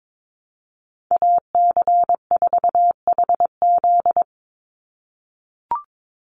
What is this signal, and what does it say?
Morse code, heard as a keyed tone, about 22 wpm, 700 Hz. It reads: AC4HZ
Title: USA Call Signs 22wpm